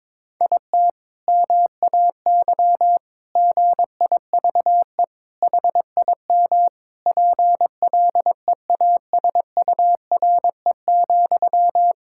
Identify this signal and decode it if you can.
Morse code, heard as a keyed tone, about 21 wpm, 700 Hz. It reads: IT MAY GIVE HIM PLEASURE,
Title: Sentences from Top 1600 Words 22wpm